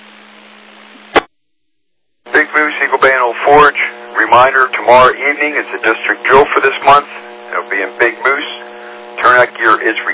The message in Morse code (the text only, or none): none